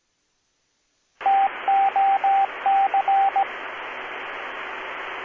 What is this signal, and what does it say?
Morse code, heard as a keyed tone, about 18 wpm, 780 Hz, with radio noise. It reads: TOC